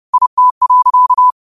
Morse code, heard as a keyed tone, about 30 wpm, 1000 Hz. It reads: IT.
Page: 30 WPM Morse code quotes